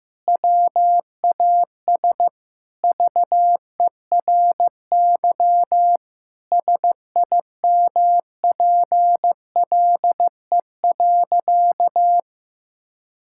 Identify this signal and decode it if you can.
Morse code, heard as a keyed tone, about 15 wpm, 700 Hz. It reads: WAS VERY SIMPLE.
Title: Sentences from Top 1000 Words - Review Part 6 - 15wpm